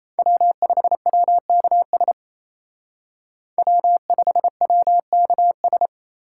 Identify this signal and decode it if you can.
Morse code, heard as a keyed tone, about 36 wpm, 700 Hz. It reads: W5WKS W5WKS